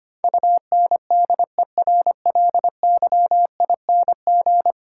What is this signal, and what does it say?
Morse code, heard as a keyed tone, about 25 wpm, 700 Hz. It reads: UNDERLYING
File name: 10 Letter Words - First 3 Letters Given - 25wpm